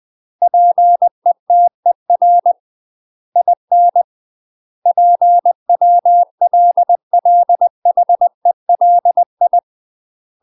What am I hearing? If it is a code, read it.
Morse code, heard as a keyed tone, about 20 wpm, 700 Hz. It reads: PETER IN PWLLHELI
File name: UK MASTER QSOs TRAINING / 20WPM / UK MASTER QSOs 0066